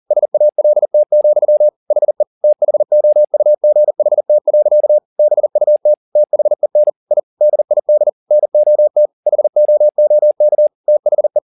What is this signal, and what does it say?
Morse code, heard as a keyed tone, about 40 wpm, 600 Hz. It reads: SALT, HE THOUGHT. BUT THEN I DID NOT HOOK THE
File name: the old man and the sea 2 40wpm